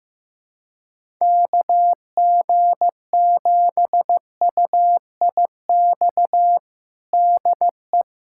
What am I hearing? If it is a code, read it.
Morse code, heard as a keyed tone, about 15 wpm, 700 Hz. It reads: KG7UIX DE
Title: USA Callsign DE Callsign - 6 Characters - 15wpm